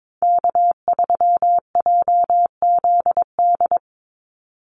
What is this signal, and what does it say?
Morse code, heard as a keyed tone, about 22 wpm, 700 Hz. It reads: K3JZD
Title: NA SOTA Chaser Callsigns 22wpm